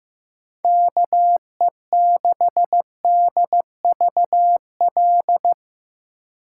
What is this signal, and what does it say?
Morse code, heard as a keyed tone, about 17 wpm, 700 Hz. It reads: KE6DVL
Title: Speed Racing - USA 2x3 Call Signs 15wpm